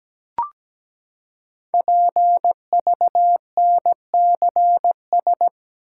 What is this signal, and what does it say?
Morse code, heard as a keyed tone, about 17 wpm, 700 Hz. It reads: PVNCS
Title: Code Groups - 5 Letter:Numbers - 17wpm